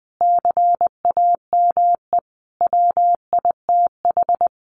Morse code, heard as a keyed tone, about 20 wpm, 700 Hz. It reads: CAME WITH
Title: Sentences from Top 1300 Words 20wpm